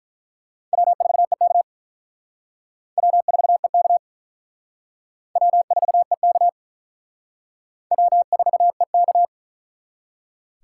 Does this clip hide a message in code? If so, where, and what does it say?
Yes: Morse code from 0.7 to 9.2 s, heard as a keyed tone, about 48 wpm, 700 Hz. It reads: W4EK W4EK W4EK W4EK